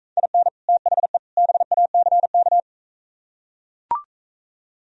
Morse code, heard as a keyed tone, about 43 wpm, 700 Hz. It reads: IN THE BACK